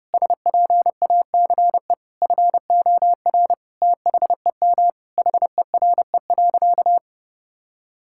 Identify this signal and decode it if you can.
Morse code, heard as a keyed tone, about 32 wpm, 700 Hz. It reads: SPACE FOR THEM HERE.
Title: Sentences from Top 1000 Words - Review Part 5 - 30wpm